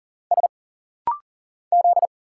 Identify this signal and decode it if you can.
Morse code, heard as a keyed tone, about 40 wpm, 700 Hz. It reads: S Z
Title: Single Letters - Rapid-Fire 40wpm